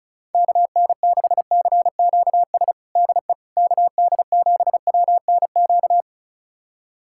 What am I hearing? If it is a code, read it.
Morse code, heard as a keyed tone, about 36 wpm, 690 Hz. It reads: KN6CQS DE KD7WNQ
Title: USA Callsign DE Callsign - 6 Characters - 35wpm